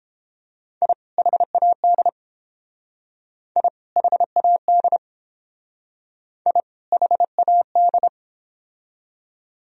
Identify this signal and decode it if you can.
Morse code, heard as a keyed tone, about 29 wpm, 700 Hz. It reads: I HAD I HAD I HAD